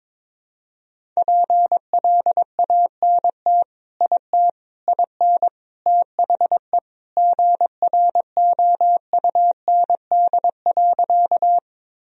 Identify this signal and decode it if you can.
Morse code, heard as a keyed tone, about 22 wpm, 700 Hz. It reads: PLANT IT IN THE GROUND.
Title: Sentences from Top 1000 Words - Review Part 1 - 22wpm